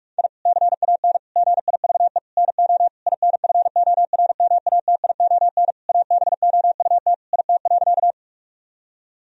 Morse code, heard as a keyed tone, about 46 wpm, 700 Hz. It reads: I CAN GIVE NO INFORMATION ABOUT IT.
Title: Sentences from Top 1000 Words - Review Part 3 - 45wpm